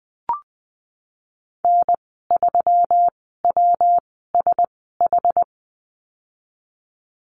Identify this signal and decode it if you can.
Morse code, heard as a keyed tone, about 20 wpm, spaced as for 15 wpm, 700 Hz. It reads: N3WSH
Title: 140 Lesson 23g - Letter f - Call Signs 20wpm